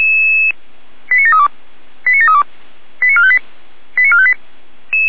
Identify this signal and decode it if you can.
Five-tone selective call (ZVEI), heard as quick runs of five tones, tones of about 70 ms. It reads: 89732 89732 89458 89458